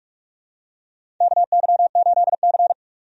Morse code, heard as a keyed tone, about 46 wpm, 700 Hz. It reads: KY8C
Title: USA Call Signs - 4 Characters - 45wpm